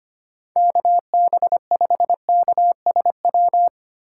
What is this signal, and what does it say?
Morse code, heard as a keyed tone, about 25 wpm, 700 Hz. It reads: KB5KSW
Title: USA Call Signs - 6 Characters - 8x Character Spacing - 25wpm